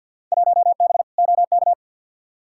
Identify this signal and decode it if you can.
Morse code, heard as a keyed tone, about 46 wpm, 700 Hz. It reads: 1B OK
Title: Field Day Exchanges Encore - 50wpm